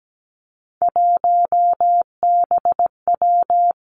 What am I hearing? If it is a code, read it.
Morse code, heard as a keyed tone, about 17 wpm, 700 Hz. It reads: 1BW
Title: Three Challenging Letter-Number Combinations 17wpm